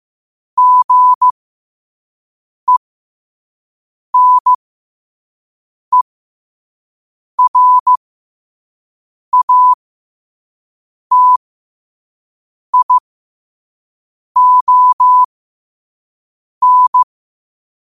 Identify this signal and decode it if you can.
Morse code, heard as a keyed tone, about 15 wpm, spaced as for 5 wpm, 1000 Hz. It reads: GENERATION
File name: qotdfiles / 5wpm